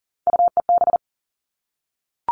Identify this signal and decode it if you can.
Morse code, heard as a keyed tone, about 39 wpm, 700 Hz. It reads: UEB
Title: All Cut Numbers - 3-Digit 40wpm